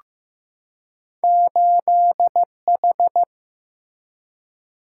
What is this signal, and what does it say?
Morse code, heard as a keyed tone, about 15 wpm, 700 Hz. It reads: 8H
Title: Single-Letter Single-Number 15wpm